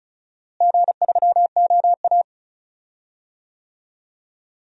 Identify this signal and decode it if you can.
Morse code, heard as a keyed tone, about 36 wpm, 700 Hz. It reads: G3OA
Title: DXCC-100 Call Signs - 4 Characters - Repeated 2x - 35wpm